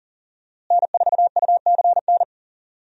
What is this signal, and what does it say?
Morse code, heard as a keyed tone, about 38 wpm, 700 Hz. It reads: N4UCN
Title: USA Call Signs - 5 Characters - Repeated 3x - 40wpm